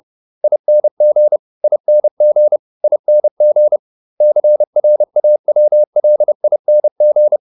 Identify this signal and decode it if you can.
Morse code, heard as a keyed tone, about 30 wpm, 600 Hz. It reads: ING ING ING CRAWLING